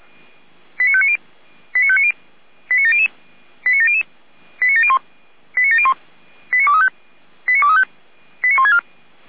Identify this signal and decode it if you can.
Five-tone selective call (ZVEI), heard as quick runs of five tones, tones of about 70 ms. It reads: 89590 89590 89700 89700 89701 89701 89235 89235 89164